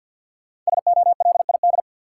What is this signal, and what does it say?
Morse code, heard as a keyed tone, about 51 wpm, 700 Hz. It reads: SOLID
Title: Five Letter Words - 50wpm